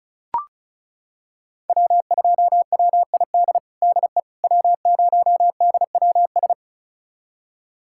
Morse code, heard as a keyed tone, about 36 wpm, 700 Hz. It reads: W2WID DE W0DWS
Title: USA Callsign DE Callsign - 5 Characters - 35wpm